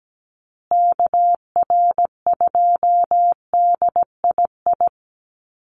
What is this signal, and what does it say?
Morse code, heard as a keyed tone, about 17 wpm, 700 Hz. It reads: KR2DII
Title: USA Call Signs Repeated 2x 17wpm